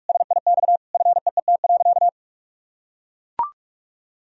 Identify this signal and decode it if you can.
Morse code, heard as a keyed tone, about 41 wpm, 700 Hz. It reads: SIX FEET.
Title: Sentences from Top 1000 Words - Review Part 2 - 45wpm